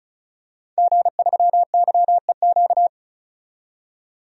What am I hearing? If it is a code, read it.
Morse code, heard as a keyed tone, about 34 wpm, 700 Hz. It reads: G3YEQ
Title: DXCC-100 Call Signs - 5 Characters - 35wpm